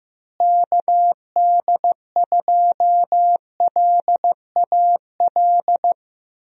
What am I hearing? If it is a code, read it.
Morse code, heard as a keyed tone, about 15 wpm, 700 Hz. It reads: KD2LAL
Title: USA Call Signs - 6 Characters - Repeated 2x - 15wpm